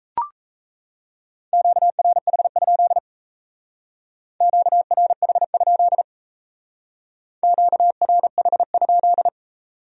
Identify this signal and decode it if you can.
Morse code, heard as a keyed tone, about 33 wpm, 700 Hz. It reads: QRH? QRH? QRH?
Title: Speed-Racing Q-Codes - 28wpm